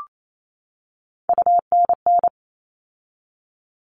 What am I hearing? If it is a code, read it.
Morse code, heard as a keyed tone, about 28 wpm, 700 Hz. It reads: UNN
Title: All Cut Numbers - 3-Digit 28wpm